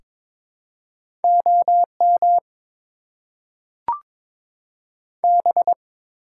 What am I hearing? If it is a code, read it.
Morse code, heard as a keyed tone, about 23 wpm, 700 Hz. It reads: OM B
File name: Top 90 QSO Elements - 6x Character Spacing - 22wpm